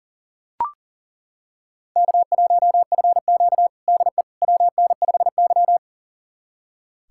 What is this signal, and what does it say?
Morse code, heard as a keyed tone, about 40 wpm, 700 Hz. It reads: K1FQ DE WN5Y
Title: USA Callsign DE Callsign - 4 Characters - 40wpm